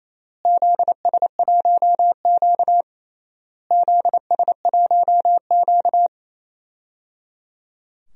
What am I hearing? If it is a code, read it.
Morse code, heard as a keyed tone, about 28 wpm, 700 Hz. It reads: ZS1Q ZS1Q
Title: DXCC-100 Call Signs - 4 Characters - Repeated 3x - 28wpm